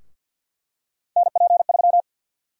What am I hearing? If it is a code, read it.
Morse code, heard as a keyed tone, about 47 wpm, 700 Hz. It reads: NP3
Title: USA Call Sign Prefixes - 50wpm